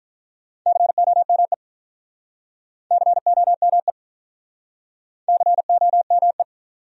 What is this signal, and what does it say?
Morse code, heard as a keyed tone, about 46 wpm, 700 Hz. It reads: COME COME COME